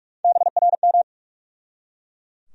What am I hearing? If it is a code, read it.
Morse code, heard as a keyed tone, about 47 wpm, 700 Hz. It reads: BRM
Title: USA Call Sign Postfixes - 3 Letters 45wpm